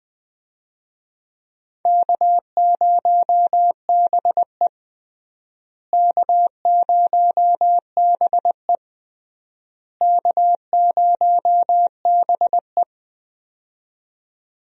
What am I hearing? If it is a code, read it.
Morse code, heard as a keyed tone, about 19 wpm, 700 Hz. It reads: K0BE K0BE K0BE
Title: USA Call Signs - 4 Characters - Repeated 3x - 20wpm